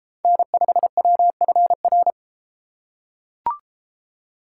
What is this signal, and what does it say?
Morse code, heard as a keyed tone, about 32 wpm, 700 Hz. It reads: N5WFR